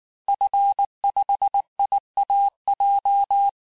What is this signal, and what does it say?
Morse code, heard as a keyed tone, about 19 wpm, 800 Hz. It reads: F5IAJ
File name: cours-du-lundi files / cours-1589-texte-FIN-de-QSO-vitesse-18mpm